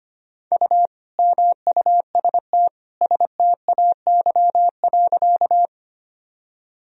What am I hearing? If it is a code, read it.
Morse code, heard as a keyed tone, about 25 wpm, 700 Hz. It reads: U MUST STAY.
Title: Sentences from Top 500 Words and Abbreviations 25wpm